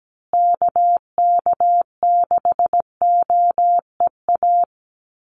Morse code, heard as a keyed tone, about 17 wpm, 700 Hz. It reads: KK6OEA
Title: USA Call Signs Repeated 2x 17wpm